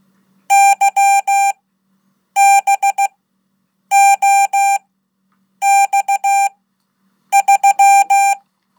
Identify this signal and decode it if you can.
Morse code, heard as a keyed tone, about 16 wpm, spaced as for 8 wpm, 790 Hz. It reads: YBOX3
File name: Code / 5min-1